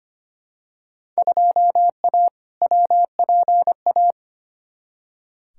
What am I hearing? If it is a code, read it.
Morse code, heard as a keyed tone, about 25 wpm, 700 Hz. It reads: 2A WPA
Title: Field Day Exchanges - 25wpm